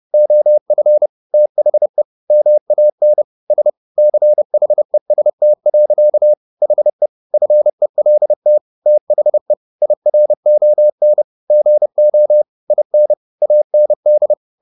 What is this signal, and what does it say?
Morse code, heard as a keyed tone, about 30 wpm, 600 Hz. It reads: OF THE MAN S CHEST. HE FELT THE IRON GO IN AND